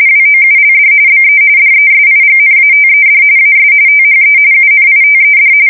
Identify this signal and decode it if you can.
RTTY, about 45.45 baud. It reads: WOULD BE DESIGNATED AS THE UNITED
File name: fsk170 / usnavy2